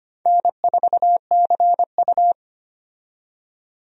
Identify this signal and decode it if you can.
Morse code, heard as a keyed tone, about 25 wpm, 700 Hz. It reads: N4CU
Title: USA Call Signs - 4 Characters - 25wpm